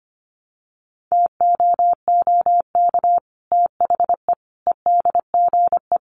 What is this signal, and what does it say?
Morse code, heard as a keyed tone, about 25 wpm, 700 Hz. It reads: TOOK THE EDGE
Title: Sentences from Top 1300 Words 25wpm